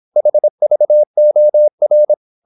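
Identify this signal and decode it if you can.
Morse code, heard as a keyed tone, about 26 wpm, 600 Hz. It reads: HVOR
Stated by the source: High-speed (QRQ) Morse at 26 wpm.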